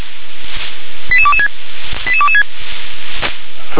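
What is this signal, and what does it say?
Five-tone selective call (ZVEI), heard as quick runs of five tones, tones of about 70 ms. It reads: 88226 88226
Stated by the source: ZVEI five-tone selective call received on SDR.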